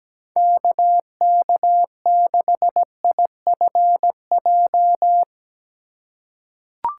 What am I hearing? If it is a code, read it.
Morse code, heard as a keyed tone, about 17 wpm, 700 Hz. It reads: KK6IFJ
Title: USA Call Signs - 3x Character Spacing - 17wpm